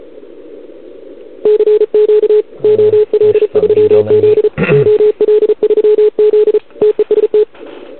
Morse code, heard as a keyed tone, about 36 wpm, 440 Hz. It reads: CQ OL3Z OL3Z TEST
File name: CQWWCW2021 / OL3Z